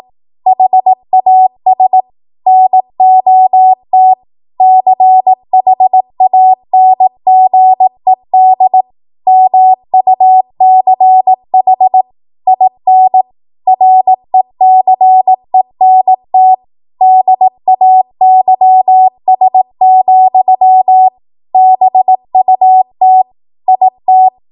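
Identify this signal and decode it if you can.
Morse code, heard as a keyed tone, about 18 wpm, 750 Hz. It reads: HAS NOT CHANGED MUCH IN RECENT DAYS, BUT IT
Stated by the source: ARRL 18 wpm Morse code practice archive transmission.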